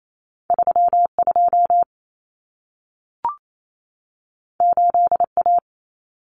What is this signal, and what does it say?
Morse code, heard as a keyed tone, about 28 wpm, 700 Hz. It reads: 32 8A
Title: Common Cut Numbers - 2-Digit 28wpm